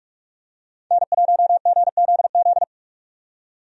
Morse code, heard as a keyed tone, about 43 wpm, 700 Hz. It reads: N1GZZ
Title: USA Call Signs - 5 Characters - 45wpm